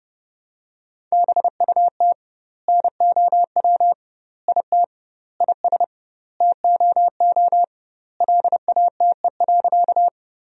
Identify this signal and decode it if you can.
Morse code, heard as a keyed tone, about 30 wpm, 700 Hz. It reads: BUT NOW IT IS TOO LATE.
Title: 2x Word-Spacing - Sentences from Top 300 Words - 30wpm